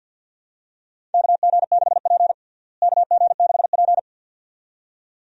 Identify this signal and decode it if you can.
Morse code, heard as a keyed tone, about 52 wpm, 700 Hz. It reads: KG6P KG6P
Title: USA Call Signs - 4 Characters - Repeated 2x - 50wpm